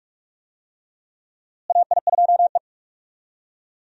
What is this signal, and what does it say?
Morse code, heard as a keyed tone, about 42 wpm, 700 Hz. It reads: AI2E
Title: USA Call Signs - 4 Characters - Repeated 3x - 45wpm